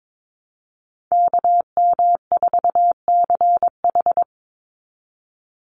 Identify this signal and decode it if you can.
Morse code, heard as a keyed tone, about 22 wpm, 700 Hz. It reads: KM4CH
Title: Call Signs Repeated 2x - 22wpm